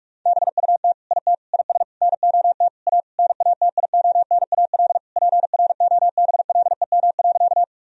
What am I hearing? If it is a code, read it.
Morse code, heard as a keyed tone, about 46 wpm, 700 Hz. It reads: BUT IT IS NOT A NATIONAL PROBLEM.
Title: Sentences from Top 1000 Words - Review Part 4 - 45wpm